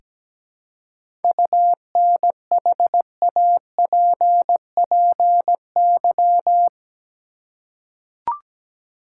Morse code, heard as a keyed tone, about 17 wpm, 700 Hz. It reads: UNHAPPY
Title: Seven Letter Words - 17wpm